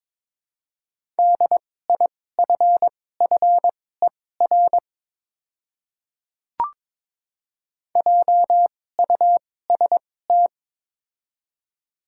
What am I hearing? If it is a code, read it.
Morse code, heard as a keyed tone, about 22 wpm, spaced as for 16 wpm, 700 Hz. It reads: DIFFER JUST
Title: 2x Character Spacing - Top 200 Words - 22wpm